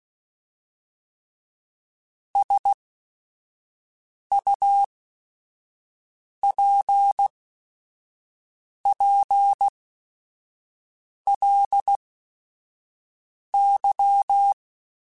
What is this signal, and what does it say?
Morse code, heard as a keyed tone, about 16 wpm, spaced as for 5 wpm, 800 Hz. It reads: SUPPLY
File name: Technical Subjects / Technical Subjects-mp3s / Sw Pwr Sup @5